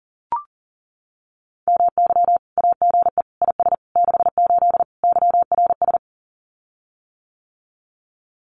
Morse code, heard as a keyed tone, about 40 wpm, 700 Hz. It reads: MY AGE IS 68 YRS
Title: Common QSO Phrases Encore 40wpm